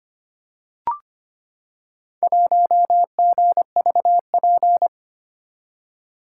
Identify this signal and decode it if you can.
Morse code, heard as a keyed tone, about 26 wpm, 700 Hz. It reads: 1GVP
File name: practice / 25wpm / Four Letter-Number Combinations - 25wpm